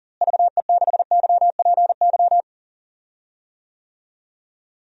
Morse code, heard as a keyed tone, about 40 wpm, 700 Hz. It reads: VE6YPY
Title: DXCC-100 Call Signs - 6 Characters - Repeated 2x - 40wpm